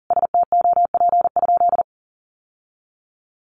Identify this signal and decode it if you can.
Morse code, heard as a keyed tone, about 40 wpm, 700 Hz. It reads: STOP?
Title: Sentences from Top 1300 Words 40wpm